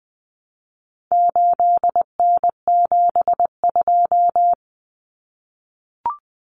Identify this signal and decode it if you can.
Morse code, heard as a keyed tone, about 20 wpm, 700 Hz. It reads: 8N72
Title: Common Cut Numbers - 4-Digit 20wpm